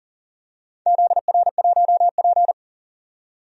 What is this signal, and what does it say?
Morse code, heard as a keyed tone, about 40 wpm, 700 Hz. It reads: ZR1P
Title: DXCC-100 Call Signs - 4 Characters - 40wpm